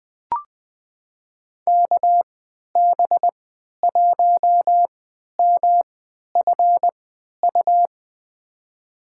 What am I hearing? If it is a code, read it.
Morse code, heard as a keyed tone, about 20 wpm, spaced as for 11 wpm, 700 Hz. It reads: KB1MFU
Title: USA Call Signs - 3x Character Spacing - 20wpm